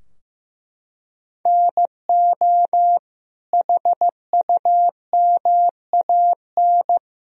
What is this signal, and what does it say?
Morse code, heard as a keyed tone, about 16 wpm, 700 Hz. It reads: NO HUMAN
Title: Sentences from Top 1000 Words - Review Part 3 - 15wpm